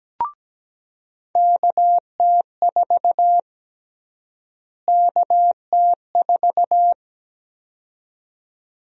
Morse code, heard as a keyed tone, about 17 wpm, 700 Hz. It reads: KT4 KT4
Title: USA Call Sign Prefixes - Repeated 2x - 17wpm